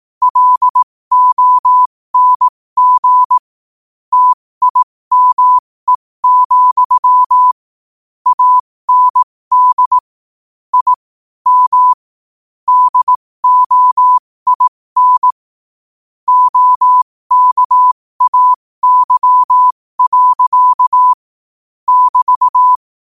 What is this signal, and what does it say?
Morse code, heard as a keyed tone, about 18 wpm, spaced as for 15 wpm, 1000 Hz. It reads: LONG TIME, AND I M DOIN OKAY. <BT>
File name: qotdfiles / 15wpm